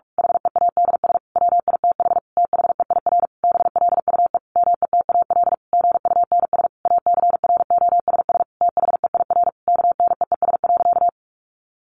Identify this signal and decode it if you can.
Morse code, heard as a keyed tone, about 46 wpm, 700 Hz. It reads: HEADS WITH THEIR BLUE METAL GUNS ACROSS THEIR KNEES.
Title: A Sound Of Thunder 45wpm